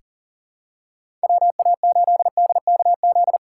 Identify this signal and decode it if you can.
Morse code, heard as a keyed tone, about 40 wpm, 700 Hz. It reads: WA8DKZ
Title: USA Call Signs - 6 Characters - 2x Character Spacing - 40wpm